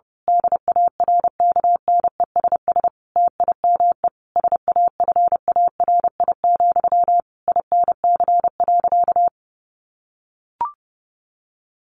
Morse code, heard as a keyed tone, about 30 wpm, 700 Hz. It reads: DARKNESS TIME SAFARI, INC.
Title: A Sound Of Thunder 30wpm